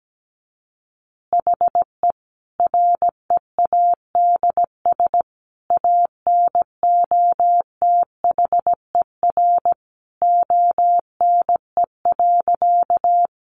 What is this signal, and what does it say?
Morse code, heard as a keyed tone, about 17 wpm, 700 Hz. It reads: HE READS ANOTHER ONE.
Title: Additional Easy Phrases 17wpm